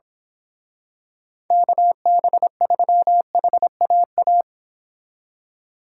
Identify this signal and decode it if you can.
Morse code, heard as a keyed tone, about 26 wpm, 700 Hz. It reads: KB3HAA
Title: Speed Racing - USA 2x3 Call Signs 22wpm